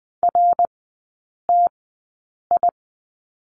Morse code, heard as a keyed tone, about 20 wpm, spaced as for 8 wpm, 700 Hz. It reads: RTI
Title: 053 Lesson 10e - Letter r - Three Characters 20wpm